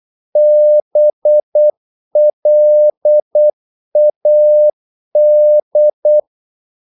Morse code, heard as a keyed tone, about 8 wpm, 600 Hz. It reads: BLAD
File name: Korte ord DK 8wpm